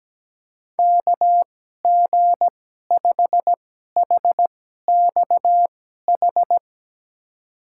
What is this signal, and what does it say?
Morse code, heard as a keyed tone, about 17 wpm, spaced as for 12 wpm, 700 Hz. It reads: KG5HXH
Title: USA Call Signs - 6 Characters - 2x Character Spacing - 17wpm